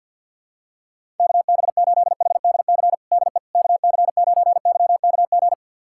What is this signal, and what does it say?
Morse code, heard as a keyed tone, about 51 wpm, 700 Hz. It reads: KB8HDC DE KC9YKG
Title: USA Callsign DE Callsign - 6 Characters - 50wpm